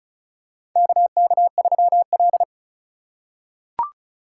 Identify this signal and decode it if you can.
Morse code, heard as a keyed tone, about 36 wpm, 700 Hz. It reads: KK3L
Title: USA Call Signs - 4 Characters - 3x Character Spacing - 35wpm